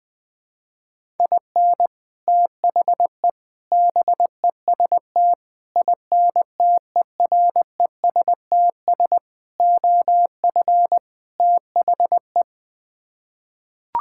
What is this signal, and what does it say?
Morse code, heard as a keyed tone, about 21 wpm, 700 Hz. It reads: IN THE BEST INTERESTS OF THE
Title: Sets of 6 Words - 20wpm